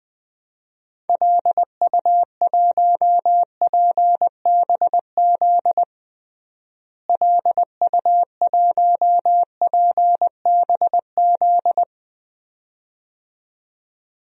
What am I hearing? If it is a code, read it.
Morse code, heard as a keyed tone, about 21 wpm, 700 Hz. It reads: LU1PBZ LU1PBZ
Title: DXCC-100 Call Signs - 6 Characters - Repeated 2x - 20wpm